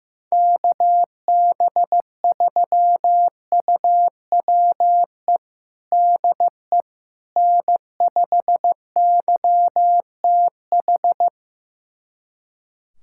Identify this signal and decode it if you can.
Morse code, heard as a keyed tone, about 15 wpm, 700 Hz. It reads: KB3UWE DE N5YTH
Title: USA Callsign DE Callsign 15wpm